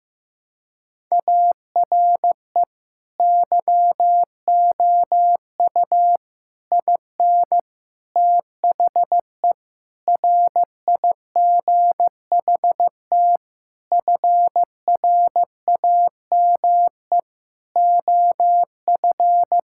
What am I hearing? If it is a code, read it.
Morse code, heard as a keyed tone, about 15 wpm, 700 Hz. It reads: ARE YOU IN THE RIGHT FRAME OF
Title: Sentences from Top 1000 Words - Review Part 6 - 15wpm